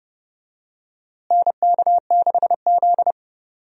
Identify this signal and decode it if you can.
Morse code, heard as a keyed tone, about 30 wpm, 700 Hz. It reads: NK6Z
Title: USA Call Signs - 4 Characters - Repeated 3x - 30wpm